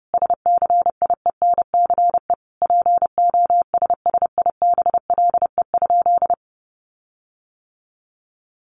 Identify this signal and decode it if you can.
Morse code, heard as a keyed tone, about 30 wpm, 700 Hz. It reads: SCIENCE POSSIBLE?
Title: Sentences from Top 1500 Words 30wpm